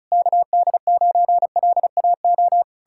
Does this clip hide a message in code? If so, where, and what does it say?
Yes: Morse code from 0.1 to 2.6 s, heard as a keyed tone, about 40 wpm, 700 Hz. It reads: KD9LAO